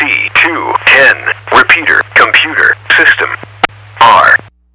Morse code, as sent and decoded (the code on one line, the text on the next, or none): none